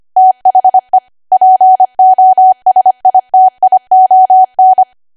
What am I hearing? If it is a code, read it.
Morse code, heard as a keyed tone, about 25 wpm, 750 Hz. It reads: THE POSITION
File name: Archive / 25 WPM / 25WPM